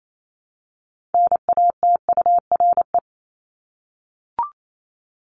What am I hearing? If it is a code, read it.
Morse code, heard as a keyed tone, about 28 wpm, 700 Hz. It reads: NATURE
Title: Top 501-750 Words 28wpm